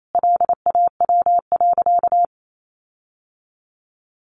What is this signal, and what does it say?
Morse code, heard as a keyed tone, about 28 wpm, 700 Hz. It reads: LAW.
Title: Sentences from Top 1400 Words 28wpm